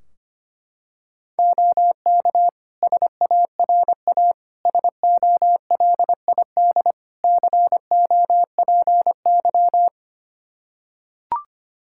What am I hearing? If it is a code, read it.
Morse code, heard as a keyed tone, about 25 wpm, 700 Hz. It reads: OK SARA SOLID COPY
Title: Practice QSO - Exchange 3 - More Info - 25wpm